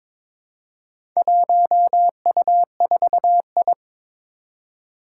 Morse code, heard as a keyed tone, about 22 wpm, 700 Hz. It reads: 1U4I
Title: Four Challenging Letter-Number Combinations 22wpm